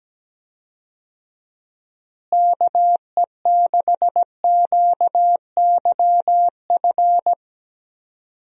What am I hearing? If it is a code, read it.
Morse code, heard as a keyed tone, about 17 wpm, 700 Hz. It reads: KE6QYF
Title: USA Call Signs - 6 Characters - Repeated 3x - 17wpm